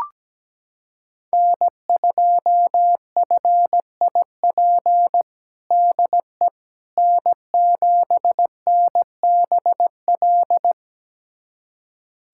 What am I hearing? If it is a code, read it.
Morse code, heard as a keyed tone, about 17 wpm, 700 Hz. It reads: N2FIP DE N7NBL
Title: USA Callsign DE Callsign - 5 Characters - 17wpm